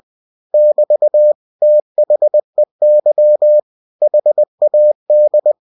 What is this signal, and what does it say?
Morse code, heard as a keyed tone, about 20 wpm, 600 Hz. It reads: <BT> THEY HAD
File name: cw-books / winnie-the-pooh / winnie-20wpm-page-0082